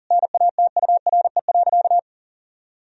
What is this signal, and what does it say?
Morse code, heard as a keyed tone, about 40 wpm, 700 Hz. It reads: NATURE.
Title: Sentences from Top 1000 Words - Review Part 3 - 40wpm